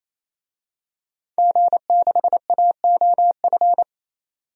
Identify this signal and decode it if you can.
Morse code, heard as a keyed tone, about 27 wpm, 700 Hz. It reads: G6AOF
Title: DXCC-100 Call Signs - 5 Characters - 28wpm